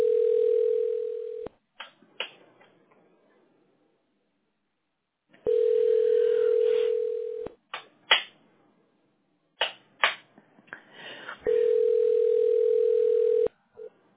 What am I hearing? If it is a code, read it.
Ringback tone (North America), 440 + 480 Hz.